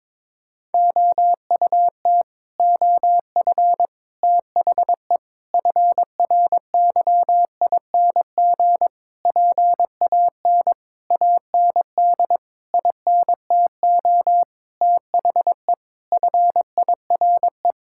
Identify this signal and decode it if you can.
Morse code, heard as a keyed tone, about 22 wpm, 700 Hz. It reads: OUT OF THE FRYING PAN AND INTO THE FIRE
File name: Sets of 9 Words - 22wpm